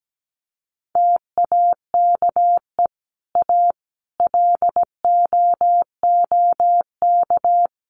Morse code, heard as a keyed tone, about 17 wpm, 700 Hz. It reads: TAKE A LOOK
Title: Sets of 5 Words 17wpm